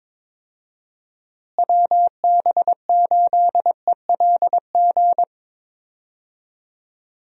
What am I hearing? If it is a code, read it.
Morse code, heard as a keyed tone, about 22 wpm, 700 Hz. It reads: WB8ELG
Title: USA Call Signs - 6 Characters - Repeated 2x - 22wpm